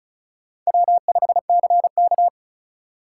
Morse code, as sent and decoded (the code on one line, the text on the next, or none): .-- ..... -.-. -.-
W5CK